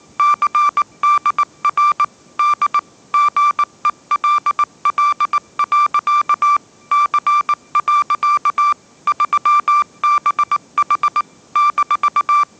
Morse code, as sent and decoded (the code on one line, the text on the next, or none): -.-. -.. .-. / -.. / --. .-.. .-.. .-.-.- / -.-. .-.-.- / ...-- -... .... / -....-
CDR D GELL. C. 3BH -